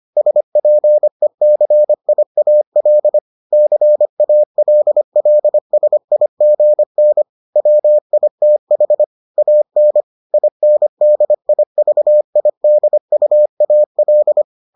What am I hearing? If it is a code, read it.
Morse code, heard as a keyed tone, about 25 wpm, 600 Hz. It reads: SPECIAL CALLSIGN WITH AN INDIVIDUAL